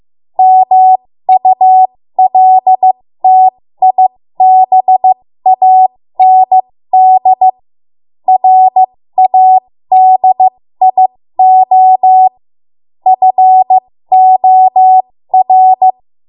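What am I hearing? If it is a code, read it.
Morse code, heard as a keyed tone, about 15 wpm, spaced as for 13 wpm, 750 Hz. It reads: MULTIBAND RADIO FOR
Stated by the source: 13 wpm Morse code practice archive transmission.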